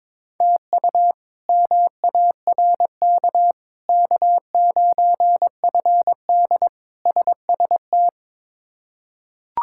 Morse code, heard as a keyed tone, about 21 wpm, 700 Hz. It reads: TU MARK K9FD SST
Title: K1USN SST QSO Practice - 22wpm